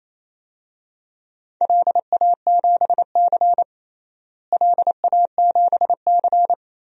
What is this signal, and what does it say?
Morse code, heard as a keyed tone, about 28 wpm, 700 Hz. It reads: LA7C LA7C
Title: DXCC-100 Call Signs - 4 Characters - Repeated 2x - 28wpm